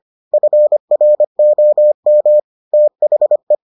Morse code, heard as a keyed tone, about 25 wpm, 600 Hz. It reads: FROM THE